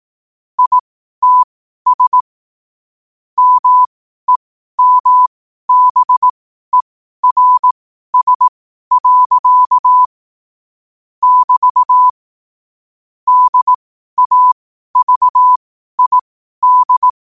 Morse code, heard as a keyed tone, about 18 wpm, spaced as for 12 wpm, 1000 Hz. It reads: ITS MEMBERS. <BT> DAVID
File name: qotdfiles / 12wpm